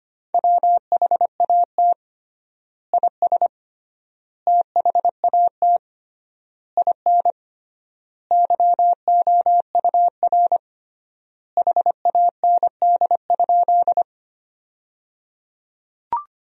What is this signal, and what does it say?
Morse code, heard as a keyed tone, about 25 wpm, 700 Hz. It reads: WHAT IS THAT IN YOUR HAND?
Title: 3x Word-Spacing - Sentences from Top 200 Words - 25wpm